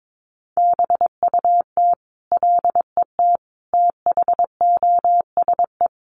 Morse code, heard as a keyed tone, about 22 wpm, 700 Hz. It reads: BUT LET THOSE
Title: Sentences from Top 1500 Words 22wpm